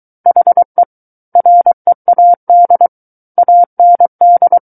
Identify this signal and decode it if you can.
Morse code, heard as a keyed tone, about 23 wpm, 700 Hz. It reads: HE READ AND
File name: Book One / 14-Chapter 14 - 23 WPM